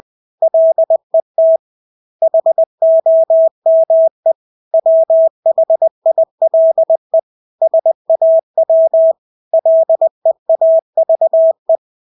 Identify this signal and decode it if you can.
Morse code, heard as a keyed tone, about 20 wpm, 650 Hz. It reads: LET HOME WHILE SAW LEAVE
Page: Engelske ord 20 wpm